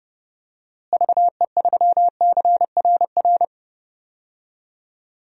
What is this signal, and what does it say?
Morse code, heard as a keyed tone, about 32 wpm, 700 Hz. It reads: VE3CRR
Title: DXCC-100 Call Signs - 6 Characters - Repeated 3x - 30wpm